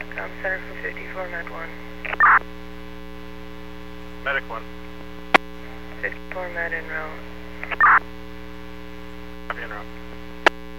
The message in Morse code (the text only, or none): none